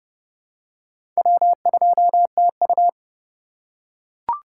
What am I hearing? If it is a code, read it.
Morse code, heard as a keyed tone, about 30 wpm, 700 Hz. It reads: W2TU
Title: USA Call Signs - 4 Characters - 2x Character Spacing - 30wpm